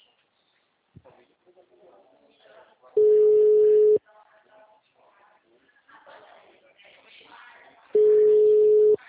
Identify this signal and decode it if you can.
Ringback tone (Europe, 425 Hz), with speech.
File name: out- -2009- - - .245